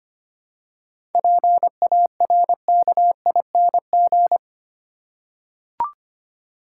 Morse code, heard as a keyed tone, about 25 wpm, 700 Hz. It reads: PARKING
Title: Seven Letter Words - 25wpm